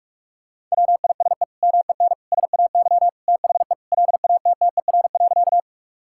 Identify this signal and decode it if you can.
Morse code, heard as a keyed tone, about 46 wpm, 700 Hz. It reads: WISE MEN SAY THE LATTER.
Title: Sentences from Top 1600 Words 45wpm